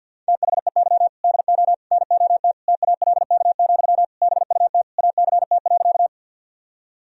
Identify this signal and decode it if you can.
Morse code, heard as a keyed tone, about 51 wpm, 700 Hz. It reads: THEY DO NOT TALK, BUT ACT.
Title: Sentences from Top 1000 Words - Review Part 1 - 50wpm